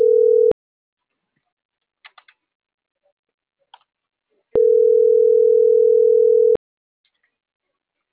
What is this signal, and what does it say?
Ringback tone (North America), 440 + 480 Hz.